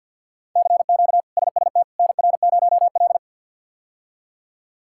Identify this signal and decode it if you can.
Morse code, heard as a keyed tone, about 47 wpm, 700 Hz. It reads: CQ SST NR0L
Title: K1USN SST QSO Practice - 50wpm